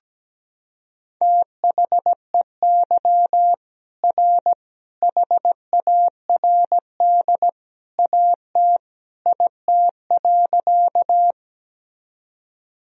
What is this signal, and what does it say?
Morse code, heard as a keyed tone, about 17 wpm, 700 Hz. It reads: THEY R HARD AT IT.
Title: Sentences from Top 300 Words and Abbreviations 17wpm